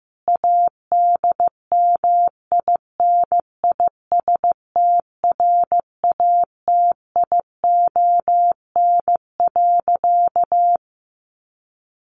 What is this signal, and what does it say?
Morse code, heard as a keyed tone, about 15 wpm, 700 Hz. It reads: ADMINISTRATION.
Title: Sentences from Top 1300 Words 15wpm